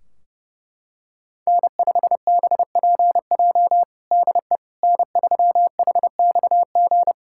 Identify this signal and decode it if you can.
Morse code, heard as a keyed tone, about 30 wpm, 700 Hz. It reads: N5BPJ DE N3HXG
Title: K1USN SST QSO Practice 30wpm